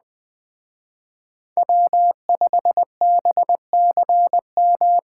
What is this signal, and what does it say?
Morse code, heard as a keyed tone, about 20 wpm, 700 Hz. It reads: W5BCM
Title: Speed Racing - USA Call Signs 17wpm